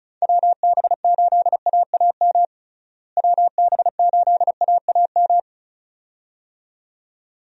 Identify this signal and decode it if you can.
Morse code, heard as a keyed tone, about 36 wpm, 700 Hz. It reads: WB8AAM WB8AAM
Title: USA Call Signs - 6 Characters - Repeated 2x - 35wpm